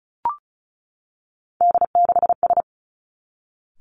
Morse code, heard as a keyed tone, about 36 wpm, 700 Hz. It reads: D6S
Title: Three Challenging Letter-Number Combinations 35wpm